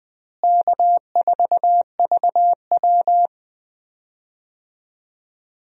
Morse code, heard as a keyed tone, about 20 wpm, 700 Hz. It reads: K4VW
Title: USA Call Signs - 4 Characters - Repeated 2x - 20wpm